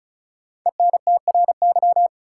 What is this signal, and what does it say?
Morse code, heard as a keyed tone, about 36 wpm, 700 Hz. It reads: ENTRY
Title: Sentences from Top 1500 Words - New Word Drill - 35wpm